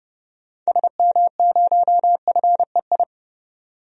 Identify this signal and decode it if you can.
Morse code, heard as a keyed tone, about 29 wpm, 700 Hz. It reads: SM0FEI
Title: DXCC-100 Call Signs - 6 Characters - 30wpm